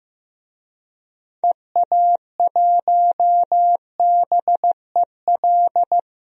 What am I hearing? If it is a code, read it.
Morse code, heard as a keyed tone, about 15 wpm, 700 Hz. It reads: EA1BEL
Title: DXCC-100 Call Signs - 6 Characters - Repeated 2x - 15wpm